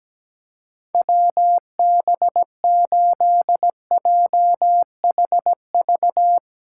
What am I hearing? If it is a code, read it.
Morse code, heard as a keyed tone, about 17 wpm, 700 Hz. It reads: WB8JHV
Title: USA Callsign DE Callsign 17wpm